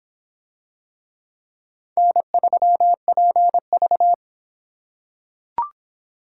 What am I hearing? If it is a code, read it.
Morse code, heard as a keyed tone, about 26 wpm, 700 Hz. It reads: N3PV